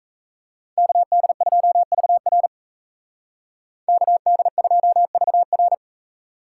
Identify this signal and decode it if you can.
Morse code, heard as a keyed tone, about 36 wpm, 700 Hz. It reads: KD2VR KD2VR